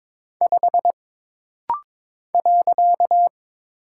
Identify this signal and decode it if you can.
Morse code, heard as a keyed tone, about 22 wpm, 700 Hz. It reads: 5 .
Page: This rapid-fire timing is geared to push your ICR (Instant Character Recognition) skills.